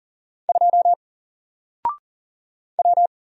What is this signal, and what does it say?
Morse code, heard as a keyed tone, about 40 wpm, 700 Hz. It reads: 2 W
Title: Single Character (Letter:Number:Punctuation:Prosign) 40wpm